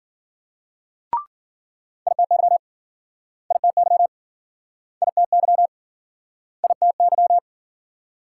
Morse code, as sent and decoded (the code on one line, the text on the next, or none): .. - -.-- / .. - -.-- / .. - -.-- / .. - -.--
ITY ITY ITY ITY